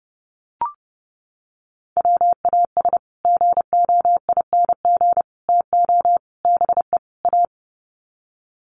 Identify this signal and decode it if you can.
Morse code, heard as a keyed tone, about 30 wpm, 700 Hz. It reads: WAS GOING TO BE A
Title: Sets of 5 Words 30wpm